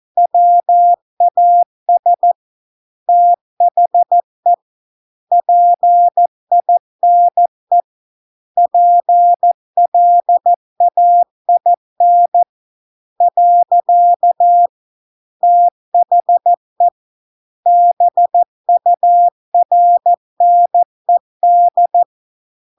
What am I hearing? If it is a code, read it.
Morse code, heard as a keyed tone, about 14 wpm, 700 Hz. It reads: WAS THE PINE PLAIN . THE BURNED